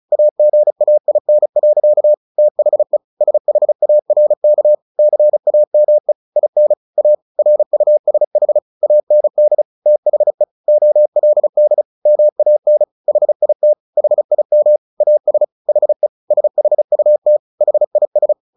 Morse code, heard as a keyed tone, about 34 wpm, 600 Hz. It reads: AGAIN. THE SHARK CAME IN A RUSH AND THE OLD MAN HIT HIM AS HE SHUT HIS
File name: the old man and the sea 2 35wpm